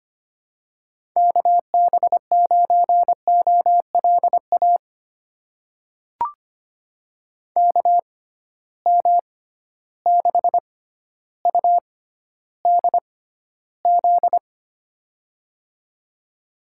Morse code, heard as a keyed tone, about 25 wpm, spaced as for 13 wpm, 700 Hz. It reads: KB9OLA KM6UDZ
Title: USA Call Signs - 6 Characters - 6x Character Spacing - 25wpm